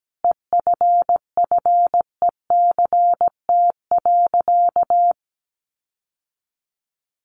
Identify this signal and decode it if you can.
Morse code, heard as a keyed tone, about 17 wpm, 700 Hz. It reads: EFFECT.
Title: Sentences from Top 1300 Words 17wpm